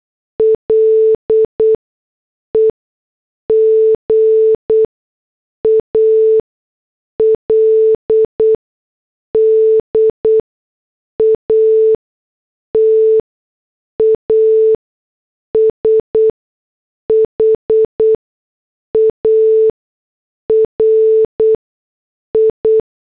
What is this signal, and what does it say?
Morse code, heard as a keyed tone, about 8 wpm, 440 Hz. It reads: L E G A L D A T A S H A R I
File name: morse / headlines